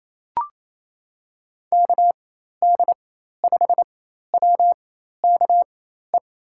Morse code, heard as a keyed tone, about 28 wpm, spaced as for 13 wpm, 700 Hz. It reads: KD5WKE
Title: USA Call Signs - 6 Characters - 4x Character Spacing - 28wpm